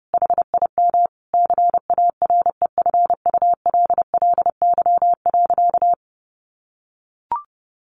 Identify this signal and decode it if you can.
Morse code, heard as a keyed tone, about 30 wpm, 700 Hz. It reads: HIM CAREFULLY.
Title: Sentences from Top 1500 Words 30wpm